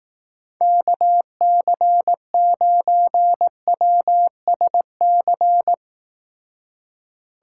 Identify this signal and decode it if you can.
Morse code, heard as a keyed tone, about 18 wpm, 700 Hz. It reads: KC9WSC
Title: Speed Racing - USA 2x3 Call Signs 15wpm